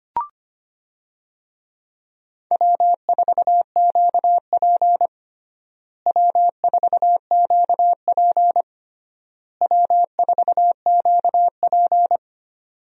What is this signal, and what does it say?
Morse code, heard as a keyed tone, about 25 wpm, 700 Hz. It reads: W4QP W4QP W4QP
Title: USA Call Signs - 4 Characters - Repeated 3x - 25wpm